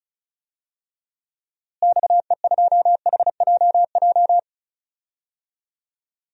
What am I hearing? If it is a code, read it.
Morse code, heard as a keyed tone, about 34 wpm, 700 Hz. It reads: XE2HJJ
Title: DXCC-100 Call Signs - 6 Characters - Repeated 3x - 35wpm